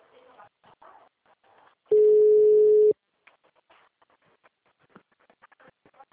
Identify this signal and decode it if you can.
Ringback tone (Europe, 425 Hz).